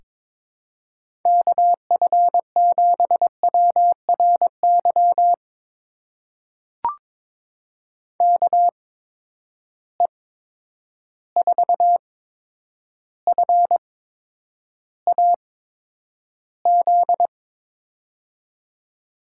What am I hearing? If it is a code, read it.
Morse code, heard as a keyed tone, about 22 wpm, spaced as for 9 wpm, 700 Hz. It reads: KF7WRY KE4FAZ
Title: USA Call Signs - 6 Characters - 8x Character Spacing - 22wpm